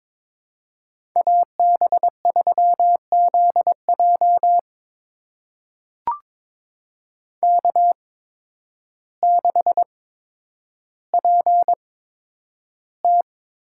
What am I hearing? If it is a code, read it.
Morse code, heard as a keyed tone, about 22 wpm, spaced as for 10 wpm, 700 Hz. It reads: AB3ZJ K6PT
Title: USA Call Signs - 8x Character Spacing - 22wpm